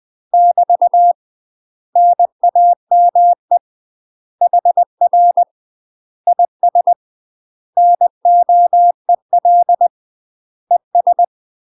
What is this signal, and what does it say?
Morse code, heard as a keyed tone, about 20 wpm, 700 Hz. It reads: <BT> NAME HR IS NOEL ES